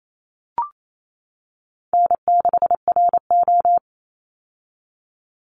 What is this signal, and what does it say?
Morse code, heard as a keyed tone, about 28 wpm, 700 Hz. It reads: N6RO
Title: Easy Call Signs - 28wpm